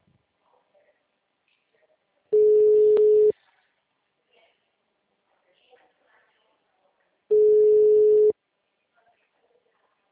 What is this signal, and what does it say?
Ringback tone (Europe, 425 Hz), with speech.